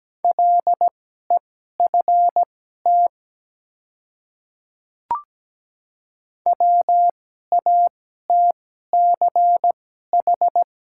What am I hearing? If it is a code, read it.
Morse code, heard as a keyed tone, about 17 wpm, spaced as for 12 wpm, 690 Hz. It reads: LEFT WATCH
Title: 2x Character Spacing - Top 300 Words - 17wpm